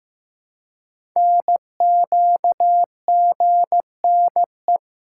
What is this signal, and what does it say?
Morse code, heard as a keyed tone, about 15 wpm, 700 Hz. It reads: NQGNE
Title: Code Groups - 5 Letters - 15wpm